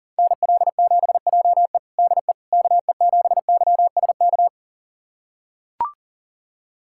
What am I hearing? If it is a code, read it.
Morse code, heard as a keyed tone, about 40 wpm, 700 Hz. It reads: NL7JE DE KE7YSK
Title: USA Callsign DE Callsign 40wpm